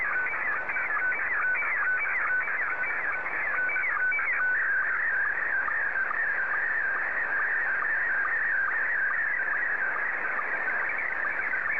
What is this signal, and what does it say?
SSTV (slow-scan television), Scottie 1.